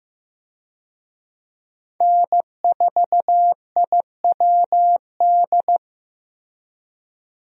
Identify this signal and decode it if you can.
Morse code, heard as a keyed tone, about 15 wpm, 700 Hz. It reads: N4IWD
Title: USA Call Signs - 5 Characters - Repeated 3x - 15wpm